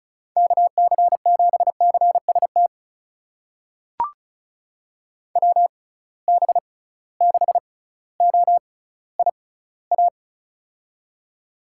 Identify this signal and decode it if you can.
Morse code, heard as a keyed tone, about 36 wpm, spaced as for 18 wpm, 700 Hz. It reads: KC7CST WB6OIA
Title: USA Call Signs - 6 Characters - 6x Character Spacing - 35wpm